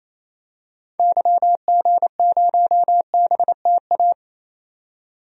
Morse code, heard as a keyed tone, about 28 wpm, 700 Hz. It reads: YG0BTA
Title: DXCC-100 Call Signs - 6 Characters - 28wpm